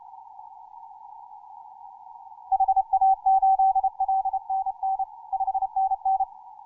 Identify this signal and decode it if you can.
Morse code, heard as a keyed tone, about 30 wpm, 770 Hz. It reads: HA8LNN 5NN
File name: simple-file-list / hk0na 10mcw